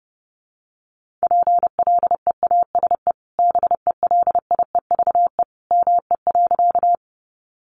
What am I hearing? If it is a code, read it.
Morse code, heard as a keyed tone, about 30 wpm, 700 Hz. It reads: PLEASE BELIEVE ME.
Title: Sentences from Top 1500 Words 30wpm